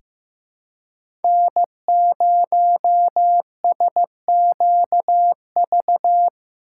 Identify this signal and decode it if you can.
Morse code, heard as a keyed tone, about 15 wpm, 700 Hz. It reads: N0SQV
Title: USA Call Signs - 5 Characters - 2x Character Spacing - 15wpm